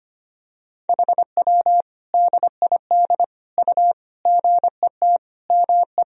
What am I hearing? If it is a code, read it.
Morse code, heard as a keyed tone, about 25 wpm, 700 Hz. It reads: HW DID U GET ME
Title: Sentences from Top 200 Words and Abbreviations 25wpm